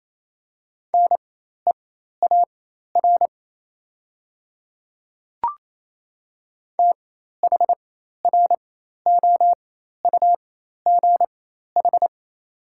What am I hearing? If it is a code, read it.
Morse code, heard as a keyed tone, about 28 wpm, spaced as for 13 wpm, 700 Hz. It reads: NEAR THROUGH
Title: 4x Character Spacing - Top 300 Words - 28wpm